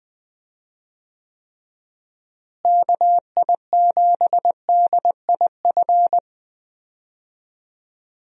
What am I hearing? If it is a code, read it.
Morse code, heard as a keyed tone, about 20 wpm, 700 Hz. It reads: KI7DIF
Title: USA Call Signs - 6 Characters - Repeated 3x - 20wpm